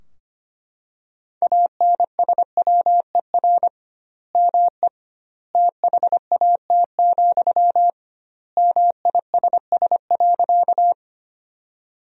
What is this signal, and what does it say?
Morse code, heard as a keyed tone, about 25 wpm, 700 Hz. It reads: ANSWER ME THAT, MISS.
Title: 2x Word-Spacing - Sentences from Top 500 Words - 25wpm